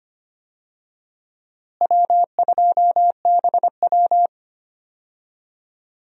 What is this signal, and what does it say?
Morse code, heard as a keyed tone, about 25 wpm, 700 Hz. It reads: W2BW
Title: USA Call Signs - 4 Characters - Repeated 3x - 25wpm